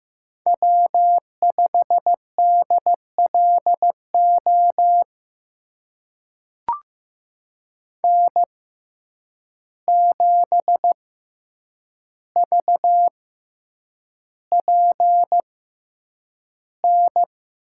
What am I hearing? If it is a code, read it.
Morse code, heard as a keyed tone, about 15 wpm, spaced as for 8 wpm, 700 Hz. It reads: W5DLO N7VPN
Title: USA Call Signs - 5 Characters - 6x Character Spacing - 15wpm